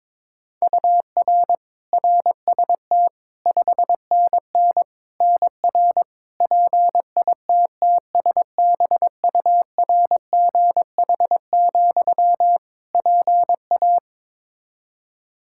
Morse code, heard as a keyed tone, about 21 wpm, 700 Hz. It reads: UR RST 5NN NR PITTSBURGH, PA
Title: Practice QSO - Exchange 1 - Beginning - 22wpm